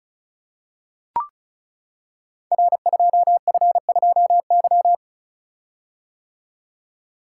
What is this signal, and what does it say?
Morse code, heard as a keyed tone, about 36 wpm, 700 Hz. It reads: R2F2Y
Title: Code Groups - 5 Letter:Numbers - 35wpm